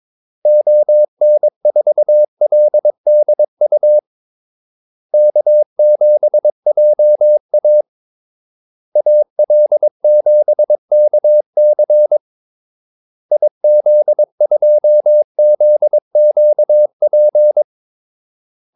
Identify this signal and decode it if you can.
Morse code, heard as a keyed tone, about 22 wpm, 600 Hz. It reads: ON4LDU K7JA AL7KC IZ2ZQP
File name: Callsigns-22wpm